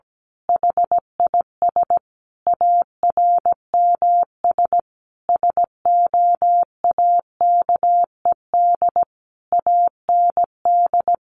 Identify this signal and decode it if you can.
Morse code, heard as a keyed tone, about 17 wpm, 700 Hz. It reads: HIS ARMS SOAKED AND
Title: A Sound Of Thunder 17wpm